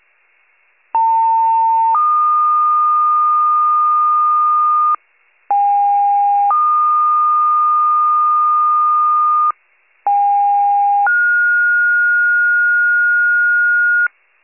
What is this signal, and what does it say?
Two-tone page, heard as a one-second tone, then a three-second tone, about 900 then 1220 Hz, about 810 then 1220 Hz, about 810 then 1430 Hz. It is heard from 0.9 to 14.1 s.
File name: pager / 73 RESCUE / wcalert